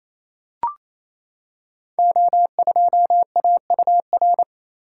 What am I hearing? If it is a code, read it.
Morse code, heard as a keyed tone, about 28 wpm, 700 Hz. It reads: O2AUR
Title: Code Groups - 5 Letter:Numbers - 28wpm